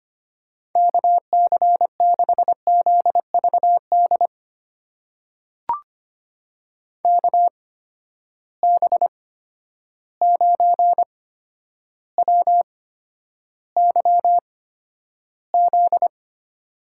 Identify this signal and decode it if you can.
Morse code, heard as a keyed tone, about 25 wpm, spaced as for 11 wpm, 700 Hz. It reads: KC6ZVD KB9WYZ
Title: USA Call Signs - 6 Characters - 8x Character Spacing - 25wpm